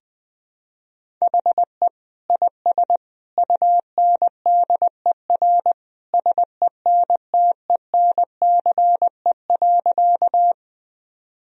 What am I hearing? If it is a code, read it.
Morse code, heard as a keyed tone, about 20 wpm, 700 Hz. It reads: HE IS UNDER SENTENCE.
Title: Sentences from Top 1000 Words - Review Part 1 - 20wpm